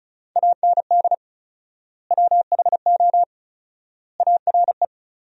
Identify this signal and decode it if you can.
Morse code, heard as a keyed tone, about 34 wpm, 700 Hz. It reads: AND WHO ARE
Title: 4x Word-Spacing - Sentences from Top 100 Words - 35wpm